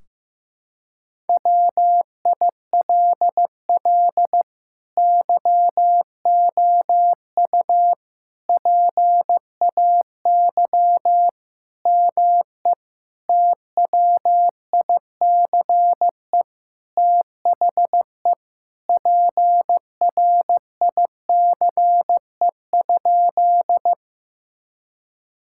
Morse code, heard as a keyed tone, about 15 wpm, 700 Hz. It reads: WILL YOU PAY ME TWICE THE PRICE?
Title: Sentences from Top 1600 Words 15wpm